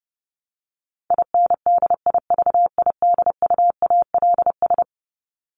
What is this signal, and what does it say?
Morse code, heard as a keyed tone, about 30 wpm, 700 Hz. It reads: INDIVIDUALS
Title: ES:S Suffix - Top 100 Words with Base Words 30wpm